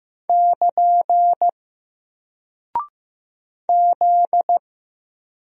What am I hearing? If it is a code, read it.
Morse code, heard as a keyed tone, about 15 wpm, 700 Hz. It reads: <KN> Z
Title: Single Character (Letter:Number:Punctuation:Prosign) 15wpm